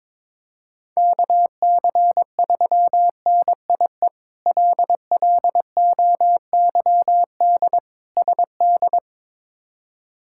Morse code, heard as a keyed tone, about 22 wpm, 700 Hz. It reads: KC3NIE LLOYD SD
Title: K1USN SST QSO Practice - 22wpm